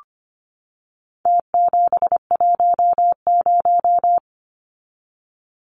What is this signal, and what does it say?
Morse code, heard as a keyed tone, about 25 wpm, 700 Hz. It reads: T710
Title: Numbers - Contest Style - 4-Digit 25wpm